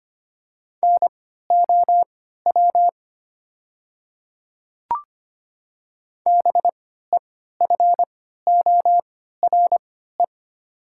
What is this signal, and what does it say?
Morse code, heard as a keyed tone, about 25 wpm, spaced as for 14 wpm, 700 Hz. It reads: NOW BEFORE
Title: 3x Character Spacing - Top 200 Words - 25wpm